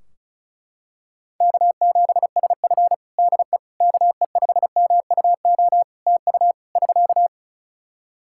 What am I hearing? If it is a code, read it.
Morse code, heard as a keyed tone, about 36 wpm, 700 Hz. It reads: K7SF DE KE5MUO TU <SK>
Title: K1USN SST QSO Practice 35wpm